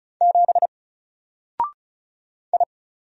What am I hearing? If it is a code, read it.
Morse code, heard as a keyed tone, about 35 wpm, 700 Hz. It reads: 7 I
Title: Single Letter-Number Alt 35wpm